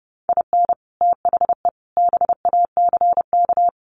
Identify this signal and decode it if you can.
Morse code, heard as a keyed tone, about 30 wpm, 700 Hz. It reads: IN THE BACK
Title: Sets of 3 Words 30wpm